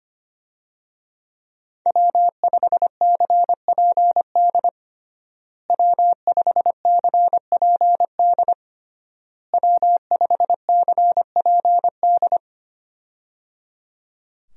Morse code, heard as a keyed tone, about 25 wpm, 700 Hz. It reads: W5CPD W5CPD W5CPD
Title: USA Call Signs - 5 Characters - Repeated 3x - 25wpm